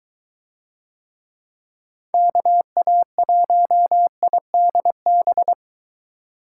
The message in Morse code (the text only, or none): KA1IDB